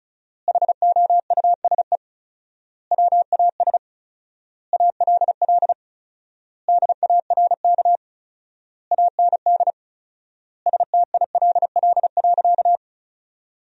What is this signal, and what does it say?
Morse code, heard as a keyed tone, about 36 wpm, 700 Hz. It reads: HOUSE WAS ALL DARK AND STILL.
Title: 4x Word-Spacing - Sentences from Top 500 Words - 35wpm